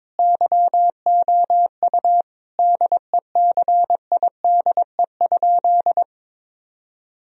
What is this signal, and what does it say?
Morse code, heard as a keyed tone, about 22 wpm, 700 Hz. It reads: YOU DECIDE?
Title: Sentences from Top 1000 Words - Review Part 3 - 22wpm